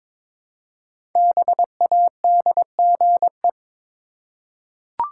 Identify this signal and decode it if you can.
Morse code, heard as a keyed tone, about 23 wpm, 700 Hz. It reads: BADGE
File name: 5 Letter Words - First 3 Letters Given - 22wpm